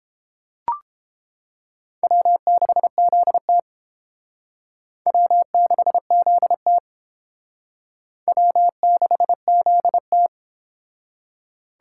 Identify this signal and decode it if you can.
Morse code, heard as a keyed tone, about 29 wpm, 700 Hz. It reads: W6ZT W6ZT W6ZT